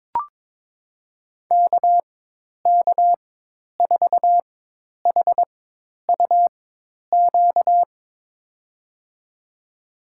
Morse code, heard as a keyed tone, about 22 wpm, spaced as for 10 wpm, 700 Hz. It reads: KK4HUQ
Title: USA Call Signs - 6 Characters - 4x Character Spacing - 22wpm